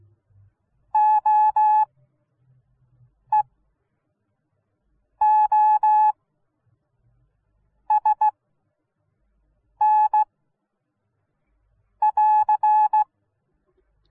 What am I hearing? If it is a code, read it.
Morse code, heard as a keyed tone, about 16 wpm, spaced as for 5 wpm, 850 Hz. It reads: OEOSN <AR>